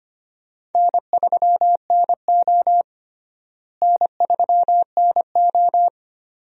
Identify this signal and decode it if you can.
Morse code, heard as a keyed tone, about 25 wpm, 700 Hz. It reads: N3NO N3NO
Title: USA Call Signs - 4 Characters - Repeated 2x - 25wpm